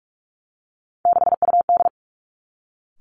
Morse code, heard as a keyed tone, about 46 wpm, 700 Hz. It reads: 6UD
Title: Three Challenging Letter-Number Combinations 45wpm